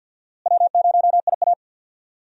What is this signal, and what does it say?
Morse code, heard as a keyed tone, about 51 wpm, 700 Hz. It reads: W0IA
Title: USA Call Signs - 4 Characters - Repeated 3x - 50wpm